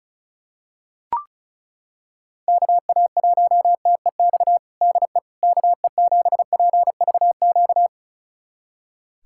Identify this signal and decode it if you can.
Morse code, heard as a keyed tone, about 36 wpm, 700 Hz. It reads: KA1TEX DE KE7PVQ
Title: USA Callsign DE Callsign 35wpm